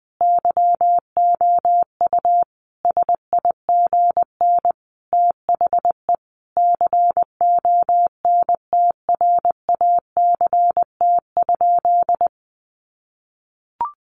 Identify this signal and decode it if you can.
Morse code, heard as a keyed tone, about 20 wpm, 700 Hz. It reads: YOU SIGN THE CONTRACT?
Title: Sentences from Top 1400 Words 20wpm